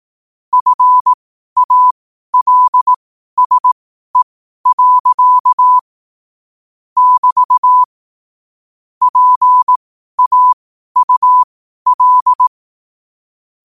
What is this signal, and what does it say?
Morse code, heard as a keyed tone, about 18 wpm, spaced as for 12 wpm, 1000 Hz. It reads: FALSE. <BT> PAUL
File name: qotdfiles / 12wpm